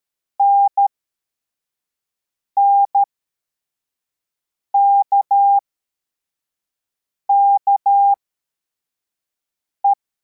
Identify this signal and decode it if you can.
Morse code, heard as a keyed tone, about 13 wpm, spaced as for 4 wpm, 800 Hz. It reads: NNKKE
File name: formation / cw / Text11